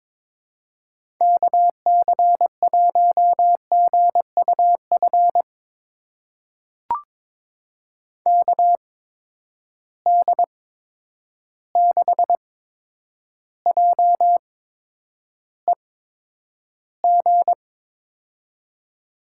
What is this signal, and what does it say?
Morse code, heard as a keyed tone, about 22 wpm, spaced as for 9 wpm, 700 Hz. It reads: KC1GUF KD6JEG
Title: USA Call Signs - 8x Character Spacing - 22wpm